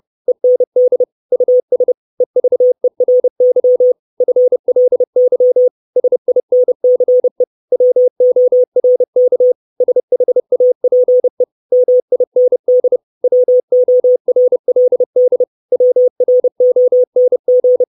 Morse code, heard as a keyed tone, about 30 wpm, 500 Hz. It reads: END US EVERY FLY SINCE WORK SHAPE MIND WORLD WRONG
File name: 632 words / 632 Words 30WPM